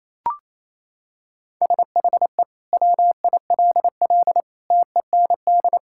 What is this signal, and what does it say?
Morse code, heard as a keyed tone, about 28 wpm, 700 Hz. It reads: SHE WILL TEND
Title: Sentences from Top 1000 Words - Review Part 5 - 28wpm